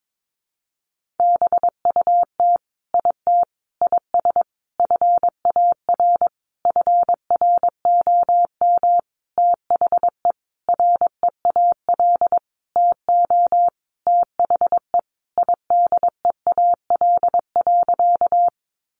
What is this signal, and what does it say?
Morse code, heard as a keyed tone, about 22 wpm, 700 Hz. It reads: BUT IT IS FAR FROM THE REAL TO THE IDEAL.
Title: Sentences from Top 1400 Words 22wpm